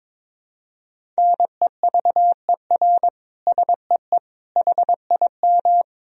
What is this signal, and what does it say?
Morse code, heard as a keyed tone, about 22 wpm, 700 Hz. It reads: NEVER SEE HIM
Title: Sentences from Top 300 Words and Abbreviations 22wpm